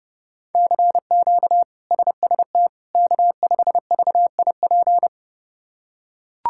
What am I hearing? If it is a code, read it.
Morse code, heard as a keyed tone, about 30 wpm, 700 Hz. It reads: CQ SST K5VIP
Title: K1USN SST QSO Practice - 30wpm